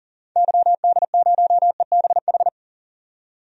Morse code, heard as a keyed tone, about 40 wpm, 700 Hz. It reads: YD0EBH
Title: DXCC-100 Call Signs - 6 Characters - 40wpm